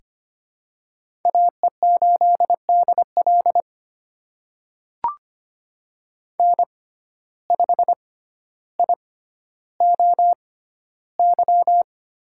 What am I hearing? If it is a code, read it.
Morse code, heard as a keyed tone, about 25 wpm, spaced as for 13 wpm, 700 Hz. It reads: AE8DL N5IOY
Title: USA Call Signs - 5 Characters - 6x Character Spacing - 25wpm